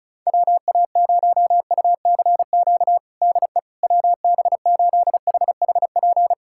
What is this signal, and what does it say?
Morse code, heard as a keyed tone, about 34 wpm, 700 Hz. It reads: WA0UCQ DE WB8HHP
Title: USA Callsign DE Callsign - 6 Characters - 35wpm